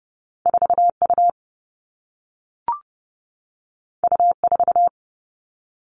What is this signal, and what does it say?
Morse code, heard as a keyed tone, about 30 wpm, 700 Hz. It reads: 4 U U 4
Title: Two Challenging Letter-Number Combinations 30wpm